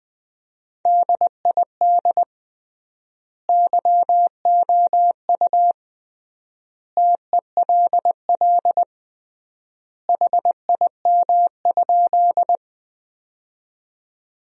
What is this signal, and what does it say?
Morse code, heard as a keyed tone, about 20 wpm, 700 Hz. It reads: DID YOU TELL HIM?
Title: 3x Word-Spacing - Sentences from Top 200 Words - 20wpm